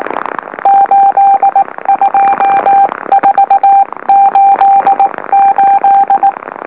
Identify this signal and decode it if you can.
Morse code, heard as a keyed tone, about 19 wpm, 790 Hz, with radio noise. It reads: 82488
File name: M04- -2345utc-LOLO